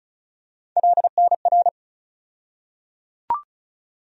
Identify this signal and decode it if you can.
Morse code, heard as a keyed tone, about 36 wpm, 700 Hz. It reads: LNR
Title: USA Call Sign Postfixes - 3 Letters 35wpm